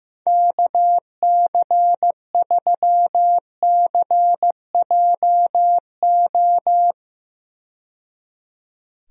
Morse code, heard as a keyed tone, about 15 wpm, 700 Hz. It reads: KC3CJO
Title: USA Call Signs - 6 Characters - 15wpm